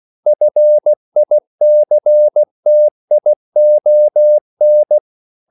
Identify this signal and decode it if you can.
Morse code, heard as a keyed tone, about 16 wpm, 600 Hz. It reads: FICTION
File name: CW / Practice / EnglishQuotations0416